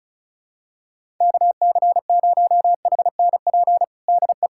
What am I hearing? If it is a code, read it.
Morse code, heard as a keyed tone, about 36 wpm, 700 Hz. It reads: KC0HNP DE
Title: USA Callsign DE Callsign 35wpm